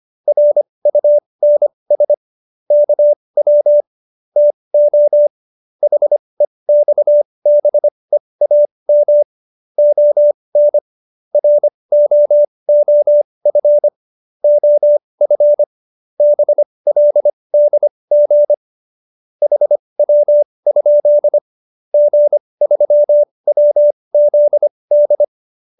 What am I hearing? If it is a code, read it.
Morse code, heard as a keyed tone, about 25 wpm, spaced as for 20 wpm, 590 Hz. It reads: RUNS KW TO HEXBEAM ON ROOF OF BLDG HW? G3WZD
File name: QSO310 20